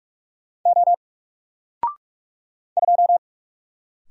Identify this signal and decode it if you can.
Morse code, heard as a keyed tone, about 45 wpm, 700 Hz. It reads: O 2
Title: Single Letter-Number 45wpm